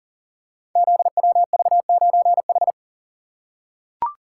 Morse code, heard as a keyed tone, about 40 wpm, 700 Hz. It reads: ZWV9H
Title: Code Groups - 5 Letter:Numbers - 40wpm